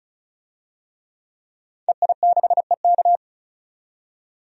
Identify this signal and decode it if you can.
Morse code, heard as a keyed tone, about 36 wpm, 700 Hz. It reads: EI6EK
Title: DXCC-100 Call Signs - 5 Characters - 35wpm